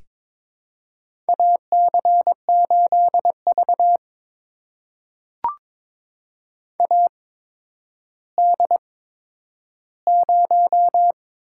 Morse code, heard as a keyed tone, about 23 wpm, spaced as for 11 wpm, 700 Hz. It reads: AC8V AD0
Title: USA Call Signs - 8x Character Spacing - 22wpm